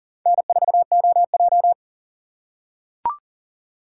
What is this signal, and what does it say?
Morse code, heard as a keyed tone, about 40 wpm, 700 Hz. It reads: N4OJ
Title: USA Call Signs - 4x Character Spacing - 40wpm